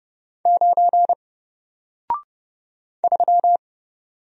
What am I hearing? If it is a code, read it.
Morse code, heard as a keyed tone, about 31 wpm, 700 Hz. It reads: 9 3
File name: Single-Letter-Number-30wpm